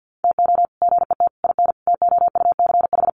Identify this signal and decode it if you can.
Morse code, heard as a keyed tone, about 50 wpm, 700 Hz. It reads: TO GET IN TOUCH
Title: Sets of 5 Words Encore2 50wpm